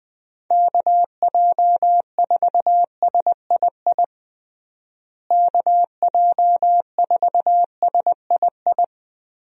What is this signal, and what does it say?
Morse code, heard as a keyed tone, about 20 wpm, 700 Hz. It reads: KJ4SII KJ4SII
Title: USA Call Signs - 6 Characters - Repeated 2x - 20wpm